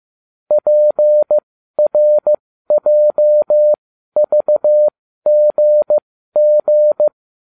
Morse code, heard as a keyed tone, about 15 wpm, spaced as for 12 wpm, 600 Hz. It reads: PRJVGG